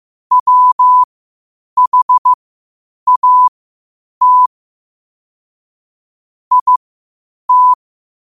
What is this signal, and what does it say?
Morse code, heard as a keyed tone, about 15 wpm, spaced as for 8 wpm, 1000 Hz. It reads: WHAT IT
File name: qotdfiles / 8wpm